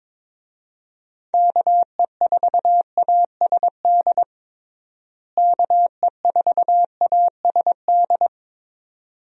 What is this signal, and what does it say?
Morse code, heard as a keyed tone, about 22 wpm, 700 Hz. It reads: KE4ASD KE4ASD
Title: USA Call Signs - 6 Characters - Repeated 2x - 22wpm